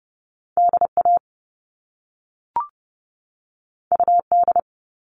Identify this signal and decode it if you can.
Morse code, heard as a keyed tone, about 30 wpm, 700 Hz. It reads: DA UD
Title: All Cut Numbers - 2-Digit 30wpm